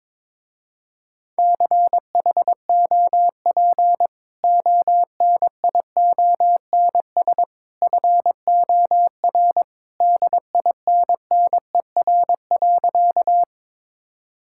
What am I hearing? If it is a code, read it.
Morse code, heard as a keyed tone, about 22 wpm, 700 Hz. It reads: CHOP ONIONS FOR DINNER.
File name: practice / 22wpm / Easy Sentences with Hints - 22wpm